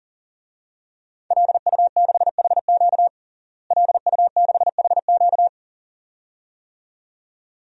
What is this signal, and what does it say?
Morse code, heard as a keyed tone, about 40 wpm, 700 Hz. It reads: LU6HQ LU6HQ
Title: DXCC-100 Call Signs - 5 Characters - Repeated 2x - 40wpm